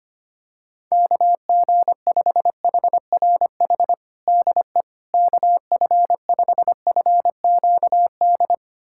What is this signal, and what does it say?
Morse code, heard as a keyed tone, about 25 wpm, 700 Hz. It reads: KG5HRH DE KF5FQD
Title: USA Callsign DE Callsign - 6 Characters - 25wpm